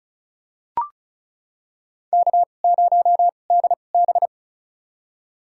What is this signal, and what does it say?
Morse code, heard as a keyed tone, about 36 wpm, spaced as for 25 wpm, 700 Hz. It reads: K0DB
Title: USA Call Signs - 4 Characters - 2x Character Spacing - 35wpm